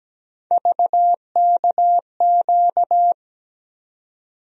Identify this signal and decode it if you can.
Morse code, heard as a keyed tone, about 17 wpm, 700 Hz. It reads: VKQ
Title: USA Call Sign Postfixes - 3 Letters 17wpm